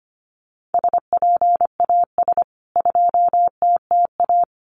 Morse code, heard as a keyed tone, about 25 wpm, 700 Hz. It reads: SPAS 2TTA
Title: CWT Exchange 25wpm